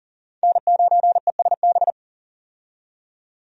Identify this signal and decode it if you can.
Morse code, heard as a keyed tone, about 40 wpm, 700 Hz. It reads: N9ESB
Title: USA Call Signs - 5 Characters - 40wpm